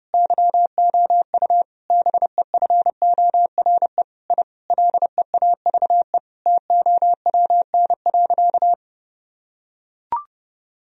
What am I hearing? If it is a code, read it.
Morse code, heard as a keyed tone, about 32 wpm, 700 Hz. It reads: YOU BEFORE I LEAVE TOWN.
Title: Sentences from Top 1000 Words - Review Part 2 - 30wpm